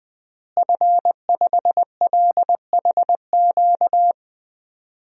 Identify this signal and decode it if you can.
Morse code, heard as a keyed tone, about 20 wpm, 700 Hz. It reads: F5LHQ
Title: DXCC-100 Call Signs - 5 Characters - Repeated 3x - 20wpm